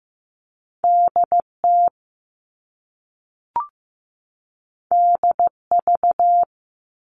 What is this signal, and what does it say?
Morse code, heard as a keyed tone, about 15 wpm, 700 Hz. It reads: DT DV
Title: All Cut Numbers - 2-Digit 15wpm